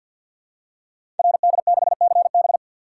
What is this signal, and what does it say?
Morse code, heard as a keyed tone, about 50 wpm, 700 Hz. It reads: RD6CB
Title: Code Groups - 5 Letter:Numbers - 50wpm